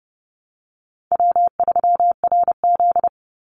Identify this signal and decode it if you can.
Morse code, heard as a keyed tone, about 30 wpm, 700 Hz. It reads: W3RZ
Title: Easy Call Signs Repeated 3x - 30wpm